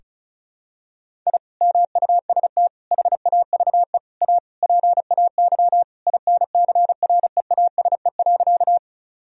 Morse code, heard as a keyed tone, about 36 wpm, 700 Hz. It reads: I MUST HAVE A PAY INCREASE.
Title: Sentences from Top 1000 Words - Review Part 4 - 35wpm